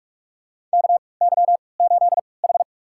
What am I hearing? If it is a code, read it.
Morse code, heard as a keyed tone, about 44 wpm, spaced as for 25 wpm, 700 Hz. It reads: KY8H
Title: USA Call Signs - 4 Characters - 3x Character Spacing - 45wpm